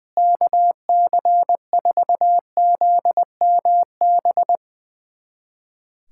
Morse code, heard as a keyed tone, about 20 wpm, 700 Hz. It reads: KC4ZMB
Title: USA Callsign DE Callsign - 6 Characters - 20wpm